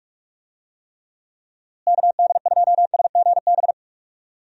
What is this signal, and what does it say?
Morse code, heard as a keyed tone, about 45 wpm, 700 Hz. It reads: KD2SGB
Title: USA Call Signs - 6 Characters - Repeated 2x - 45wpm